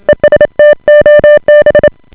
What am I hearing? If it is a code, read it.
Morse code, heard as a keyed tone, about 27 wpm, 590 Hz. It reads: ESTOB